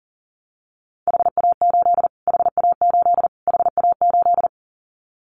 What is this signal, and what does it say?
Morse code, heard as a keyed tone, about 40 wpm, 700 Hz. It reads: HA8 HA8 HA8
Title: Callsign Prefixes for Top 25 DXCC - Repeated 3x - 40wpm